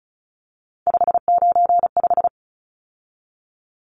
Morse code, heard as a keyed tone, about 34 wpm, 700 Hz. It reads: 595
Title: Numbers - 3-Digit 35wpm